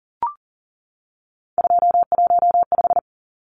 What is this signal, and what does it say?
Morse code, heard as a keyed tone, about 40 wpm, 700 Hz. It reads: 215
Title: Numbers - 3-Digit 40wpm